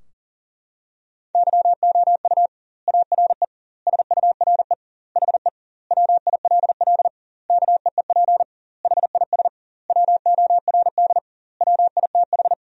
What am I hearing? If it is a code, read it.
Morse code, heard as a keyed tone, about 40 wpm, 700 Hz. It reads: YOU ARE SURE HE WILL KEEP HIS WORD WITH
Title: 2x Word-Spacing - Sentences from Top 300 Words - 40wpm